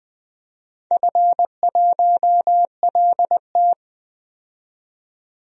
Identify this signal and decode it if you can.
Morse code, heard as a keyed tone, about 20 wpm, 700 Hz. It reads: F1LT
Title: DXCC-100 Call Signs - 4 Characters - 20wpm